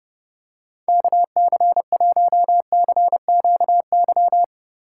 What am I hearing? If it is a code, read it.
Morse code, heard as a keyed tone, about 30 wpm, 700 Hz. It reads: KC1CQY
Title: USA Call Signs - 6 Characters - 2x Character Spacing - 30wpm